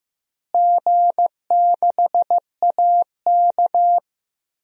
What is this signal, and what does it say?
Morse code, heard as a keyed tone, about 15 wpm, 700 Hz. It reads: G6AK
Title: DXCC-100 Call Signs - 4 Characters - Repeated 2x - 15wpm